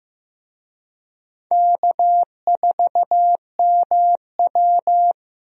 Morse code, heard as a keyed tone, about 15 wpm, 700 Hz. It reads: K4MW
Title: USA Call Signs - 4 Characters - 15wpm